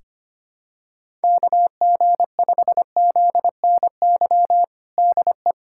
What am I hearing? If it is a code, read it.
Morse code, heard as a keyed tone, about 25 wpm, 700 Hz. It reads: KG5ZNY DE
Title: USA Callsign DE Callsign - 6 Characters - 25wpm